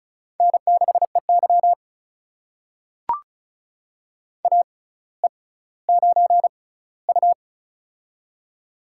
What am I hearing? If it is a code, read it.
Morse code, heard as a keyed tone, about 36 wpm, spaced as for 18 wpm, 700 Hz. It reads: N6EY AE9U
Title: USA Call Signs - 4 Characters - 6x Character Spacing - 35wpm